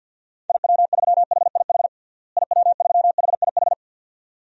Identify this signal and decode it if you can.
Morse code, heard as a keyed tone, about 53 wpm, 700 Hz. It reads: IW3HIH IW3HIH
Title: DXCC-100 Call Signs - 6 Characters - Repeated 2x - 50wpm